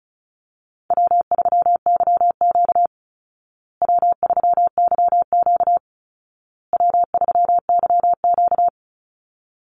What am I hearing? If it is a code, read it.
Morse code, heard as a keyed tone, about 34 wpm, 700 Hz. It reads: W3YQ W3YQ W3YQ
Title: Call Signs Repeated 2x - 35wpm